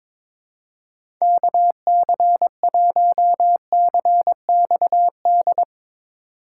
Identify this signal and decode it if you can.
Morse code, heard as a keyed tone, about 23 wpm, 700 Hz. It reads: KC1CXD
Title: USA Call Signs - 6 Characters - 4x Character Spacing - 22wpm